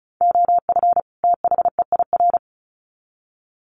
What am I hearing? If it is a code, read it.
Morse code, heard as a keyed tone, about 34 wpm, 700 Hz. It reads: OF THEIR
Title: Sets of 2 Words 35wpm